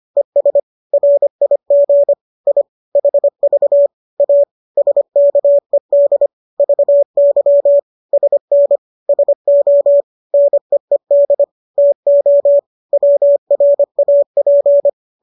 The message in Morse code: ES RIG I HV A SKED VY SN SO NEED TO WRAP

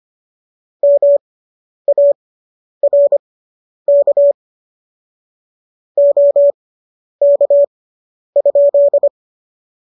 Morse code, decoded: MARK OK?

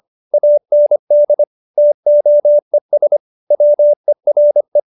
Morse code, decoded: AND TOES WERE

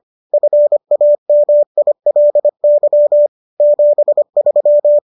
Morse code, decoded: FAMILY 73

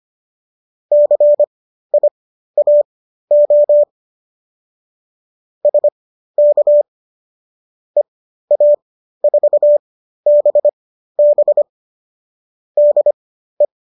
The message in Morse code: CIAO SK EA4BB DE

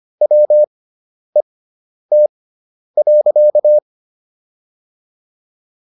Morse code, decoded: WET.

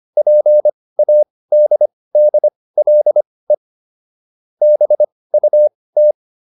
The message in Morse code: PADDLE BUT

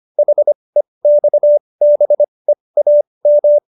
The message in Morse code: HEXBEAM

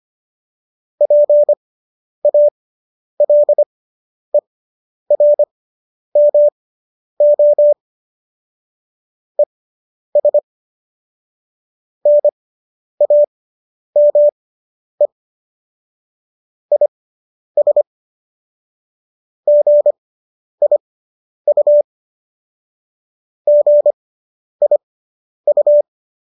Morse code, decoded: PALERMO ES NAME IS GIU GIU

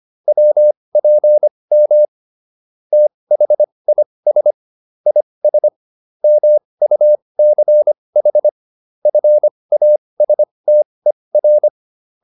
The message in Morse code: WPM THIS IS MUCH FASTER

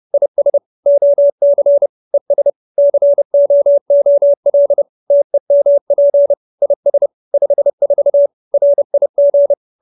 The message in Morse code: IS OC ES COOL TEMP IS 54 RIG

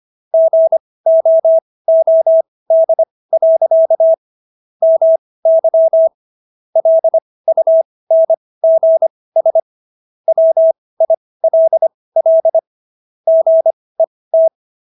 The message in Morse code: GOOD. MY LUNGS WILL GET

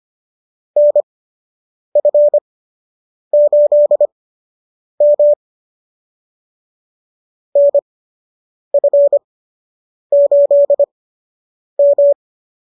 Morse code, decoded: NF8M NF8M

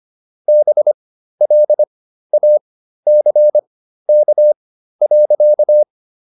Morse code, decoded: BLACK.

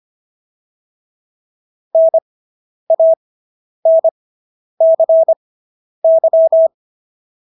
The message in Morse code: NANCY

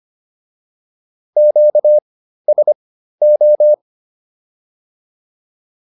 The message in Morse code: QSO

